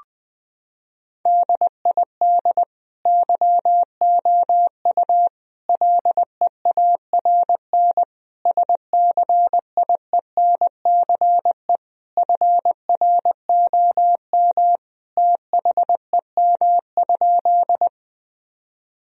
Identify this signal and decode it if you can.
Morse code, heard as a keyed tone, about 21 wpm, 700 Hz. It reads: DID YOU LEARN SCIENCE FROM THEM?
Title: Sentences from Top 1000 Words - Review Part 2 - 20wpm